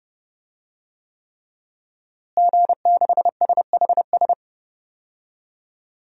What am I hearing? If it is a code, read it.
Morse code, heard as a keyed tone, about 32 wpm, 700 Hz. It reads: G6SHS
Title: DXCC-100 Call Signs - 5 Characters - Repeated 3x - 30wpm